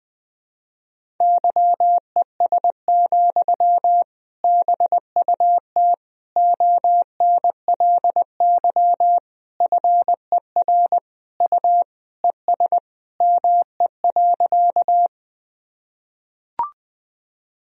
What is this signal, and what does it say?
Morse code, heard as a keyed tone, about 20 wpm, 700 Hz. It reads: YES, BUT ONLY FER U ES ME.
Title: Sentences from Top 500 Words and Abbreviations 20wpm